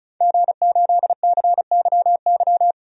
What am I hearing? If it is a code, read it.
Morse code, heard as a keyed tone, about 36 wpm, 700 Hz. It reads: G8CYY
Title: DXCC-100 Call Signs - 5 Characters - 35wpm